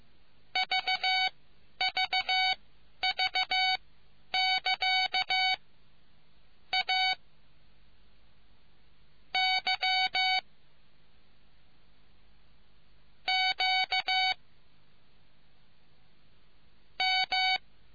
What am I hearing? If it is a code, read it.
Morse code, heard as a keyed tone, about 15 wpm, spaced as for 10 wpm, 750 Hz. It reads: VVV<KA> A Y Q M